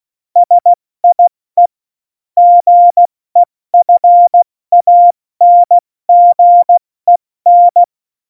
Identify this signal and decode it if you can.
Morse code, heard as a keyed tone, about 16 wpm, 700 Hz. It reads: SIE GEFANGEN
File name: Morsekurs / Witze / Witz 12. Januar 2021, 70 F80